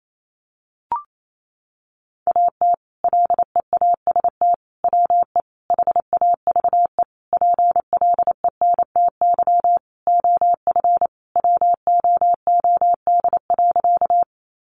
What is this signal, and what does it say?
Morse code, heard as a keyed tone, about 28 wpm, 700 Hz. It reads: AT LEAST WE HAVE PLENTY OF WOOD.
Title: Sentences from Top 1300 Words 28wpm